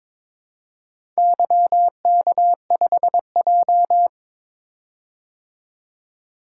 Morse code, heard as a keyed tone, about 22 wpm, 700 Hz. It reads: YK5J
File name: practice / 22wpm / Four Letter-Number Combinations - 22wpm